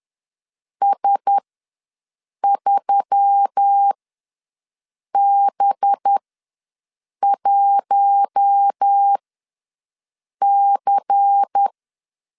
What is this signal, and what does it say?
Morse code, heard as a keyed tone, about 10 wpm, spaced as for 6 wpm, 800 Hz. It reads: S3B1C
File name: LEZIONI / 8 YJ45 / 8 YJ4 ws 10wpm 4020